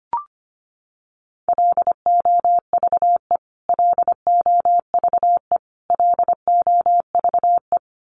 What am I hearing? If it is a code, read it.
Morse code, heard as a keyed tone, about 25 wpm, 700 Hz. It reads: LOVE LOVE LOVE
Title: Top 500 Words Repeated 25wpm